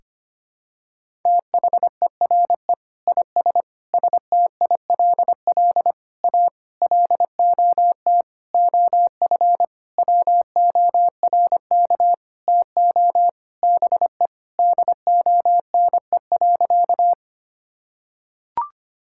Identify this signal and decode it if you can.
Morse code, heard as a keyed tone, about 25 wpm, 700 Hz. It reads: THERE IS STILL A LOT OF WORK TO BE DONE.
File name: practice / 25wpm / Sentences with 9-Word Phrases - 25wpm